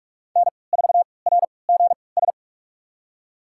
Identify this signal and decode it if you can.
Morse code, heard as a keyed tone, about 46 wpm, spaced as for 25 wpm, 700 Hz. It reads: N4RGS
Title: USA Call Signs - 3x Character Spacing - 45wpm